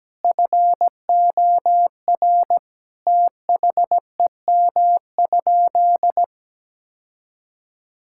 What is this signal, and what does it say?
Morse code, heard as a keyed tone, about 17 wpm, 700 Hz. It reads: FOR THEM?
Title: Sentences from Top 1000 Words - Review Part 5 - 17wpm